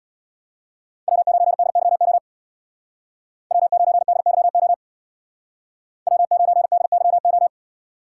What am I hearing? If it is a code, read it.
Morse code, heard as a keyed tone, about 64 wpm, 700 Hz. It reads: W9DYO W9DYO W9DYO